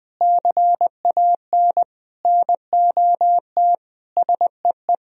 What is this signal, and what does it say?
Morse code, heard as a keyed tone, about 20 wpm, 700 Hz. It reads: CAN NOT SEE
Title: Sentences from Top 1000 Words - Review Part 3 - 20wpm